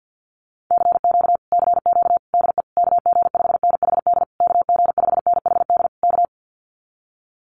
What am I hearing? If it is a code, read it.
Morse code, heard as a keyed tone, about 50 wpm, 700 Hz. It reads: CQ CQ DE KG5NHD KG5NHD K
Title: Common QSO Phrases Encore 50wpm